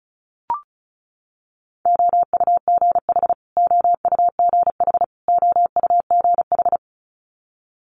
Morse code, heard as a keyed tone, about 36 wpm, 700 Hz. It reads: OUGH OUGH OUGH
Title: Multi-letter Phonemes Repeated 3x 35wpm